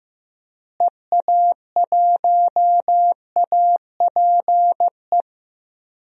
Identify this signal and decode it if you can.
Morse code, heard as a keyed tone, about 15 wpm, 700 Hz. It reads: EA1APE
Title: DXCC-100 Call Signs - 6 Characters - Repeated 2x - 15wpm